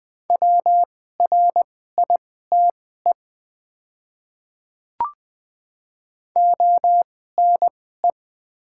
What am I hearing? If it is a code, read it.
Morse code, heard as a keyed tone, about 20 wpm, spaced as for 15 wpm, 700 Hz. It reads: WRITE ONE
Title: 2x Character Spacing - Top 100 Words - 20wpm